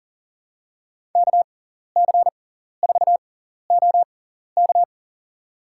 Morse code, heard as a keyed tone, about 40 wpm, spaced as for 14 wpm, 700 Hz. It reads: KC4OK
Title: USA Call Signs - 6x Character Spacing - 40wpm